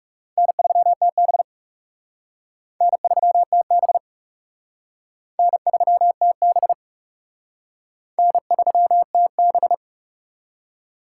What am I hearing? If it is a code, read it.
Morse code, heard as a keyed tone, about 36 wpm, 700 Hz. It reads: N3TB N3TB N3TB N3TB
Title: Speed Racing - USA 1x2 Call Signs 30wpm